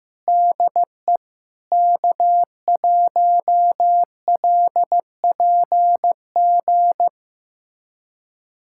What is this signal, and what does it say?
Morse code, heard as a keyed tone, about 16 wpm, 700 Hz. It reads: DE K1LPG
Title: USA Callsign DE Callsign - 5 Characters - 15wpm